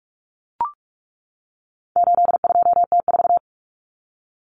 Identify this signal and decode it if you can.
Morse code, heard as a keyed tone, about 44 wpm, 700 Hz. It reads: 82T4
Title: Common Cut Numbers - 4-Digit 45wpm